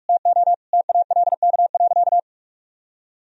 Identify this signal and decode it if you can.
Morse code, heard as a keyed tone, about 46 wpm, 700 Hz. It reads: TO TALK.
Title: Sentences from Top 400 Words and Abbreviations 45wpm